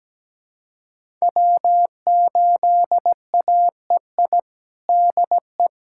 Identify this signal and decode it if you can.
Morse code, heard as a keyed tone, about 17 wpm, 700 Hz. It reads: W8AEI DE
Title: USA Callsign DE Callsign - 5 Characters - 17wpm